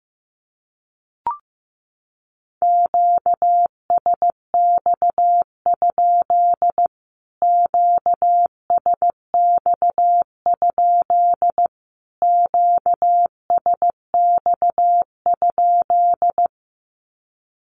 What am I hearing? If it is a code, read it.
Morse code, heard as a keyed tone, about 15 wpm, 700 Hz. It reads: QSX? QSX? QSX?
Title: Q-Codes Repeated 3x 15wpm